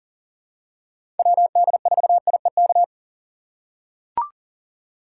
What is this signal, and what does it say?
Morse code, heard as a keyed tone, about 40 wpm, 700 Hz. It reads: WD4IEK
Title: USA Call Signs - 4x Character Spacing - 40wpm